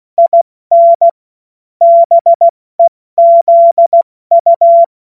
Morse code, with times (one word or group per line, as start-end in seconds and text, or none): 0.2-1.1 IN
1.8-4.8 BEZU